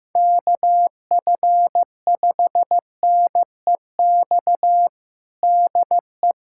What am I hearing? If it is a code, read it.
Morse code, heard as a keyed tone, about 16 wpm, 700 Hz. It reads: KF5NEX DE
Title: USA Callsign DE Callsign 15wpm